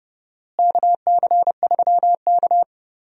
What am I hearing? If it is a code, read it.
Morse code, heard as a keyed tone, about 31 wpm, 700 Hz. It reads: KC3K
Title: USA Call Signs - 4 Characters - Repeated 2x - 30wpm